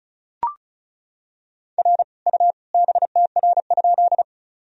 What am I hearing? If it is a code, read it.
Morse code, heard as a keyed tone, about 36 wpm, 700 Hz. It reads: R U BTR?
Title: Sentences from Top 400 Words and Abbreviations 35wpm